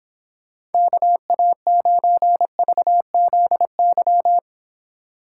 Morse code, heard as a keyed tone, about 31 wpm, 700 Hz. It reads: KA9VZY